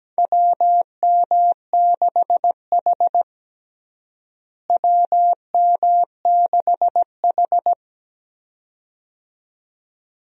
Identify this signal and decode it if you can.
Morse code, heard as a keyed tone, about 17 wpm, 700 Hz. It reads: WM6H WM6H
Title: USA Call Signs - 4 Characters - Repeated 2x - 17wpm